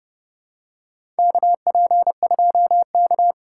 Morse code, heard as a keyed tone, about 30 wpm, 700 Hz. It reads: KP2K
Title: USA Call Signs - 4 Characters - 2x Character Spacing - 30wpm